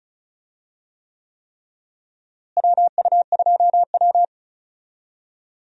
Morse code, heard as a keyed tone, about 34 wpm, 700 Hz. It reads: WU2W
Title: USA Call Signs - 4 Characters - Repeated 3x - 35wpm